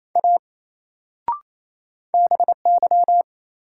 Morse code, heard as a keyed tone, about 27 wpm, 700 Hz. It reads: A BY
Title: Sentences from Top 100 Words - New Word Drill - 28wpm